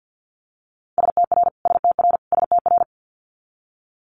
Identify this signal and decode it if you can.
Morse code, heard as a keyed tone, about 50 wpm, 700 Hz. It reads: STR STR STR
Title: Multi-letter Phonemes Repeated 3x 50wpm